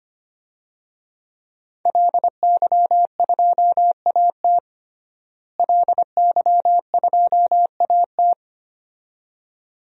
Morse code, heard as a keyed tone, about 25 wpm, 700 Hz. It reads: LY2AT LY2AT
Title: DXCC-100 Call Signs - 5 Characters - Repeated 2x - 25wpm